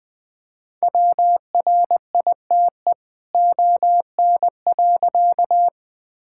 Morse code, heard as a keyed tone, about 20 wpm, 700 Hz. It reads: WRITE ON.
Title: All-Age Appropriate Jokes - 20wpm